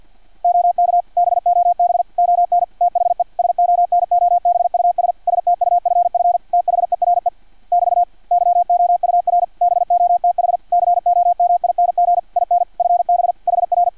Morse code, heard as a keyed tone, about 50 wpm, 700 Hz, with radio noise. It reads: OK BOB ON THE SONOBUS STUFF THERE <BT> YOUR BOTH COMING IN FB HR
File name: archived / CRSnet